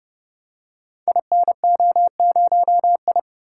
Morse code, heard as a keyed tone, about 30 wpm, 700 Hz. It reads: INO0I
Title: Code Groups - 5 Letter:Numbers - 30wpm